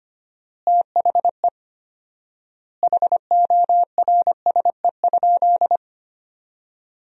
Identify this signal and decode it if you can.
Morse code, heard as a keyed tone, about 25 wpm, 700 Hz. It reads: THE HORSE?
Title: 4x Word-Spacing - Sentences from Top 500 Words - 25wpm